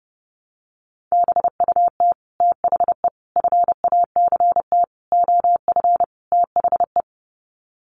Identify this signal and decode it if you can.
Morse code, heard as a keyed tone, about 30 wpm, 700 Hz. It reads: BUT THE FACT OF THE
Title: Sets of 5 Words Encore2 30wpm